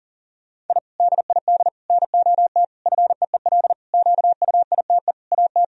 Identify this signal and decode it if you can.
Morse code, heard as a keyed tone, about 40 wpm, 700 Hz. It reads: I DID NOT FEEL QUITE AT
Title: Sentences from Top 1000 Words - Review Part 1 - 40wpm